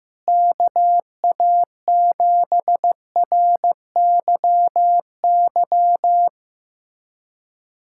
Morse code, heard as a keyed tone, about 15 wpm, 700 Hz. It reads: KA7RYY
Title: USA Callsign DE Callsign - 6 Characters - 15wpm